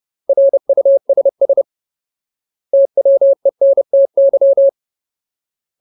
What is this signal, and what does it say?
Morse code, heard as a keyed tone, about 30 wpm, 550 Hz. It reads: RUSS TWENTY
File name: Ultimate Word List / 30WPM Ultimate